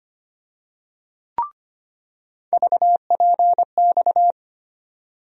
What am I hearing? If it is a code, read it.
Morse code, heard as a keyed tone, about 25 wpm, 700 Hz. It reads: VPX
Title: USA Call Sign Postfixes - 3 Letters 25wpm